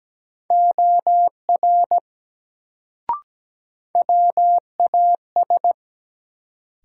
Morse code, heard as a keyed tone, about 17 wpm, 700 Hz. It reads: OR WAS
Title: Sentences from Top 100 Words - New Word Drill - 17wpm